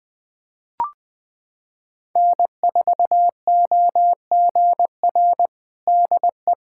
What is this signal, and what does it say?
Morse code, heard as a keyed tone, about 20 wpm, 700 Hz. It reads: N4OGR DE
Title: USA Callsign DE Callsign - 5 Characters - 20wpm